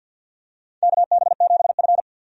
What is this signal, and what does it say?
Morse code, heard as a keyed tone, about 53 wpm, 700 Hz. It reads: KB7F
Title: USA Call Signs - 4 Characters - 50wpm